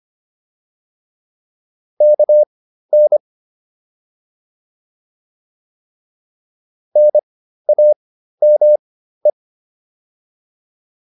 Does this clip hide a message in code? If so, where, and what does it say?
Yes: Morse code from 2.0 to 9.3 s, heard as a keyed tone, about 25 wpm, spaced as for 13 wpm, 600 Hz. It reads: KN NAME